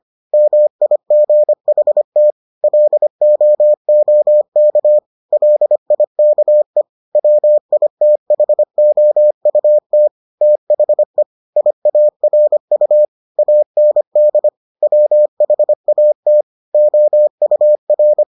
Morse code, heard as a keyed tone, about 25 wpm, 600 Hz. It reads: MIGHT LOOK LIKE WITHOUT THE IARU AND WHAT OUR